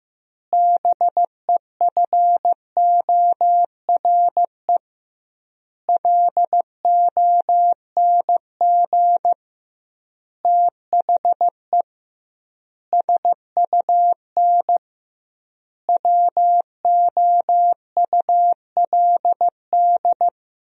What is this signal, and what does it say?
Morse code, heard as a keyed tone, about 15 wpm, 700 Hz. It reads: BEFORE LONG THE SUN WOULD
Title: 2x Word-Spacing - Sentences from Top 400 Words - 15wpm